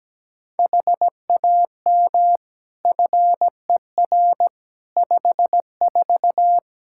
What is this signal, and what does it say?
Morse code, heard as a keyed tone, about 17 wpm, 700 Hz. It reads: HAM FER 54
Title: Practice QSO - Exchange 3 - More Info - 17wpm